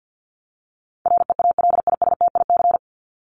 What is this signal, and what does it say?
Morse code, heard as a keyed tone, about 50 wpm, 700 Hz. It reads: REALISTIC
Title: Top 1751-2000 Words 50wpm